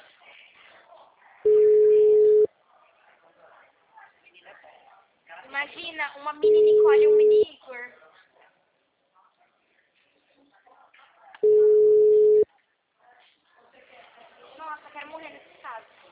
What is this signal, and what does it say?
Ringback tone (Europe, 425 Hz), with speech.